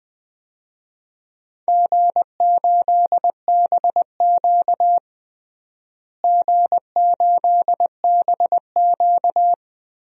Morse code, heard as a keyed tone, about 20 wpm, 700 Hz. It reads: G8BQ G8BQ
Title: DXCC-100 Call Signs - 4 Characters - Repeated 3x - 20wpm